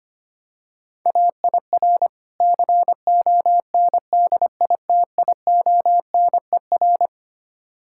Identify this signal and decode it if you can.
Morse code, heard as a keyed tone, about 25 wpm, 700 Hz. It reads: AIR CONDITIONER
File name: Easy Words - Speed-Racing with Hints - 25wpm